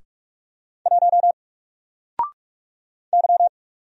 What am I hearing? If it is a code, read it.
Morse code, heard as a keyed tone, about 44 wpm, 700 Hz. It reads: 1 Y
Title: Single Character (Letter:Number:Punctuation:Prosign) 45wpm